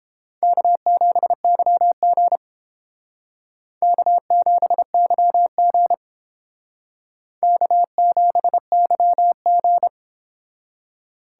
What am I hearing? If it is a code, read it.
Morse code, heard as a keyed tone, about 30 wpm, 700 Hz. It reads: K7YG K7YG K7YG